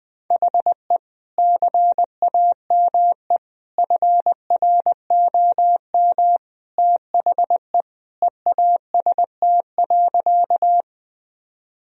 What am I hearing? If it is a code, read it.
Morse code, heard as a keyed tone, about 20 wpm, 700 Hz. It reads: HE CAME FROM THE EAST.
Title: Sentences from Top 1000 Words - Review Part 3 - 20wpm